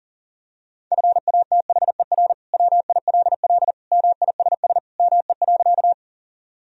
Morse code, heard as a keyed tone, about 40 wpm, 700 Hz. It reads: FATHER WILL MISS ME.
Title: Sentences from Top 1000 Words - Review Part 3 - 40wpm